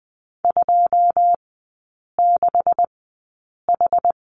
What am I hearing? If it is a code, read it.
Morse code, heard as a keyed tone, about 20 wpm, spaced as for 8 wpm, 700 Hz. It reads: 26H
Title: 133 Lesson 22e - Question Mark - Three Characters 20wpm